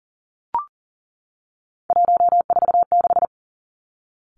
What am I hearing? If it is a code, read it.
Morse code, heard as a keyed tone, about 40 wpm, 700 Hz. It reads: 146
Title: Three Challenging Letter-Number Combinations 40wpm